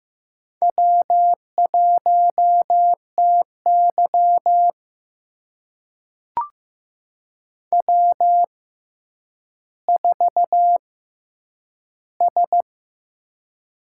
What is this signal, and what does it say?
Morse code, heard as a keyed tone, about 15 wpm, spaced as for 9 wpm, 700 Hz. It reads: W1TY W4S
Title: USA Call Signs - 4 Characters - 6x Character Spacing - 15wpm